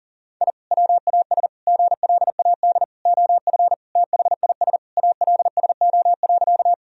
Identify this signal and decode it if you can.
Morse code, heard as a keyed tone, about 40 wpm, 700 Hz. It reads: I WAS GLAD OF THIS ALSO.
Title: Sentences from Top 1000 Words - Review Part 1 - 40wpm